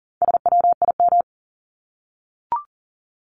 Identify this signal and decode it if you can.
Morse code, heard as a keyed tone, about 40 wpm, 700 Hz. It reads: SWIM
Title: Top 1751-2000 Words 40wpm